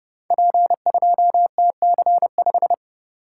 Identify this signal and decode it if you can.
Morse code, heard as a keyed tone, about 30 wpm, 700 Hz. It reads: P2TC5
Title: Code Groups - 5 Letter:Numbers - 30wpm